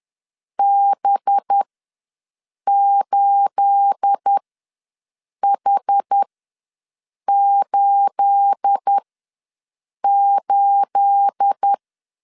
Morse code, heard as a keyed tone, about 10 wpm, spaced as for 6 wpm, 800 Hz. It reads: B8H88
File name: LEZIONI / 9 28 ws 10wpm 4124